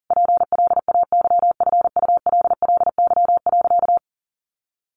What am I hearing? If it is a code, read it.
Morse code, heard as a keyed tone, about 37 wpm, 700 Hz. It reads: PLAYFULLY.
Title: A Sound Of Thunder 40wpm